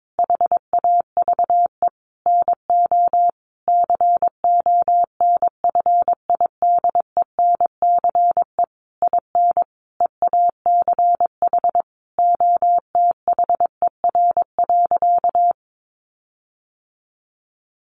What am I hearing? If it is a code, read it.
Morse code, heard as a keyed tone, about 22 wpm, 700 Hz. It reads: HAVE NO CONFIDENCE IN EACH OTHER.
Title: Sentences from Top 1500 Words 22wpm